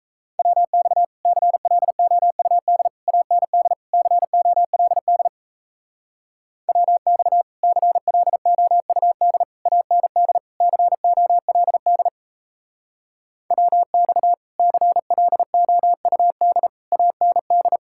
Morse code, heard as a keyed tone, about 36 wpm, 700 Hz. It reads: WX CLOUD AND COLD WX CLOUD AND COLD WX CLOUD AND